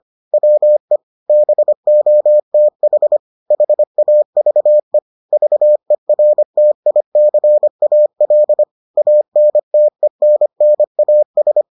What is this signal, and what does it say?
Morse code, heard as a keyed tone, about 25 wpm, 600 Hz. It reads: WE BOTH HAVE VERTICAL ANTENNAS